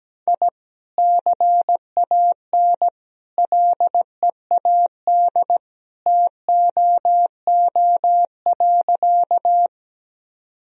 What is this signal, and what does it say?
Morse code, heard as a keyed tone, about 17 wpm, 700 Hz. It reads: I CAN LEAD TOO.
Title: Sentences from Top 1000 Words - Review Part 2 - 17wpm